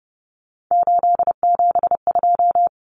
Morse code, heard as a keyed tone, about 30 wpm, 700 Hz. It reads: 872
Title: Common Cut Numbers - 3-Digit 30wpm